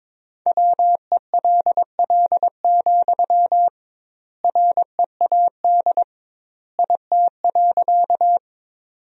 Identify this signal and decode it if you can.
Morse code, heard as a keyed tone, about 22 wpm, 700 Hz. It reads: WELL, READ IT.
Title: 2x Word-Spacing - Sentences from Top 200 Words - 22wpm